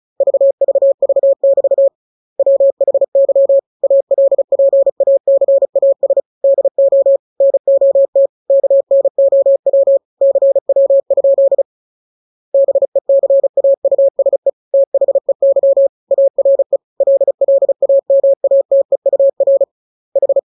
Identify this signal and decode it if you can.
Morse code, heard as a keyed tone, about 34 wpm, 550 Hz. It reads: VVV <BT> WHY ALPACAS DO NOT KNOW CW? BECAUSE THEY ARE LLAMATEUR H